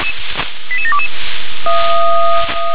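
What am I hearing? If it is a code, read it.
Five-tone selective call (ZVEI), heard as quick runs of five tones, tones of about 70 ms. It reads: 88722
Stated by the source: ZVEI five-tone selective call 88500 received by SDR.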